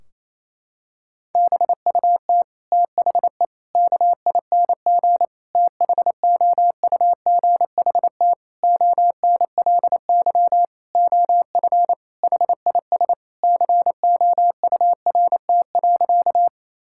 Morse code, heard as a keyed tone, about 28 wpm, 700 Hz. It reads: BUT THE KING THOUGHT ONLY OF HIS COURT.
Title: Sentences from Top 1600 Words 28wpm